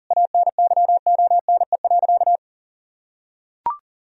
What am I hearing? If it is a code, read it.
Morse code, heard as a keyed tone, about 37 wpm, 700 Hz. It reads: ANYONE.
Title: Sentences from Top 1000 Words - Review Part 2 - 40wpm